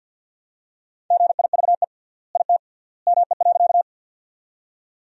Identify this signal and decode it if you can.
Morse code, heard as a keyed tone, about 50 wpm, 700 Hz. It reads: GIVE IT ME.
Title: 3x Word-Spacing - Sentences from Top 200 Words - 50wpm